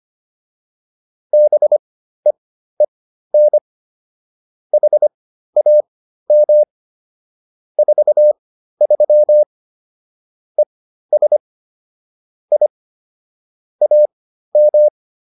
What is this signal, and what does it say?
Morse code, heard as a keyed tone, about 25 wpm, spaced as for 13 wpm, 600 Hz. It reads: BEEN HAM 43 ES I AM
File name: QSO310 13